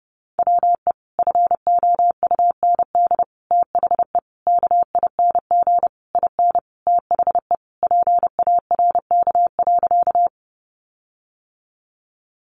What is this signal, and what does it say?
Morse code, heard as a keyed tone, about 30 wpm, 700 Hz. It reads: WE FOUND THE KING IN THE PARK.
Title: Sentences from Top 1300 Words 30wpm